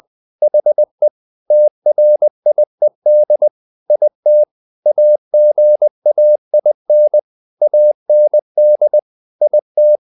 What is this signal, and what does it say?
Morse code, heard as a keyed tone, about 20 wpm, 600 Hz. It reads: HE TRIED IT AGAIN AND IT
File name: the old man and the sea part 2 20wpm